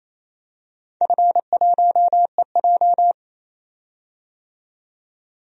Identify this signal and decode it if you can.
Morse code, heard as a keyed tone, about 27 wpm, 700 Hz. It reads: F1EJ
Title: DXCC-100 Call Signs - 4 Characters - 28wpm